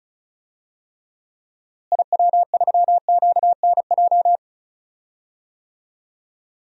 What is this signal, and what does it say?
Morse code, heard as a keyed tone, about 34 wpm, 700 Hz. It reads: IW3QNJ
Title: DXCC-100 Call Signs - 6 Characters - Repeated 3x - 35wpm